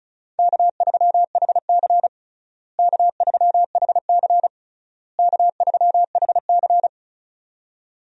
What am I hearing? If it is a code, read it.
Morse code, heard as a keyed tone, about 36 wpm, 700 Hz. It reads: K3HC K3HC K3HC
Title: USA Call Signs - 4 Characters - Repeated 3x - 35wpm